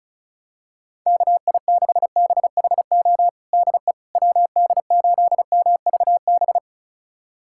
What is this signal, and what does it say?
Morse code, heard as a keyed tone, about 34 wpm, 700 Hz. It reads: KI6BHO DE WD8MVB
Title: USA Callsign DE Callsign 35wpm